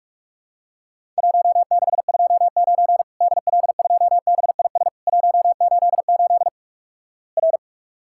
Morse code, heard as a keyed tone, about 46 wpm, 700 Hz. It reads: 1629 DL2BIS 188 R
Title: Worked All Europe DX Contest - QTC Exchange - 45wpm